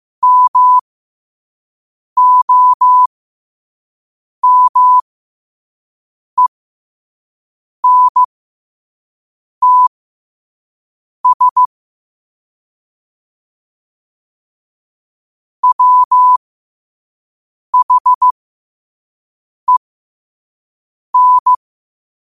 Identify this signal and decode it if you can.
Morse code, heard as a keyed tone, about 15 wpm, spaced as for 5 wpm, 1000 Hz. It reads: MOMENTS WHEN
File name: qotdfiles / 5wpm